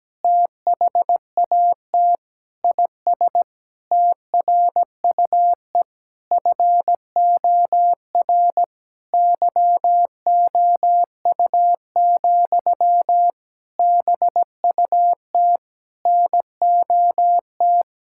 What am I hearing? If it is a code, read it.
Morse code, heard as a keyed tone, about 17 wpm, 700 Hz. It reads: THAT IS TRUE FOR YOU, BUT NOT
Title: Sentences from Top 1000 Words - Review Part 5 - 17wpm